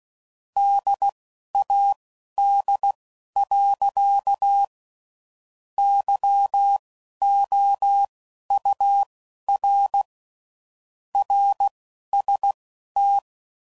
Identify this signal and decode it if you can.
Morse code, heard as a keyed tone, about 16 wpm, spaced as for 11 wpm, 800 Hz. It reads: DAD. YOUR RST
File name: CW Code Groups @10WPM / QSOs / QSO WD0FBT